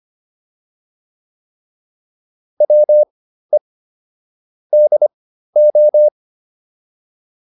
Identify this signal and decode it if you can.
Morse code, heard as a keyed tone, about 25 wpm, spaced as for 13 wpm, 600 Hz. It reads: WE DO